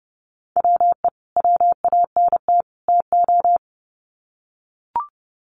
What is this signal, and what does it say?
Morse code, heard as a keyed tone, about 30 wpm, 700 Hz. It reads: WE WANT TO
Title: Sets of 3 Words 30wpm